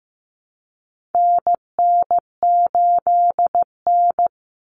Morse code, heard as a keyed tone, about 15 wpm, 700 Hz. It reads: NN8N
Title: Common Cut Numbers - 4-Digit 15wpm